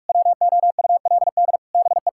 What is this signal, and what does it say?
Morse code, heard as a keyed tone, about 41 wpm, 700 Hz. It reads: WOULD BE
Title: Sentences from Top 1600 Words 45wpm